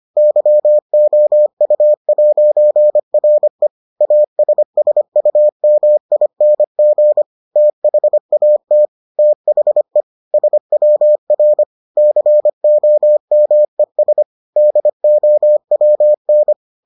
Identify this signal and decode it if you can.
Morse code, heard as a keyed tone, about 25 wpm, 600 Hz. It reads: YOU'RE ASSUMING THAT THE SWR COMES DOWN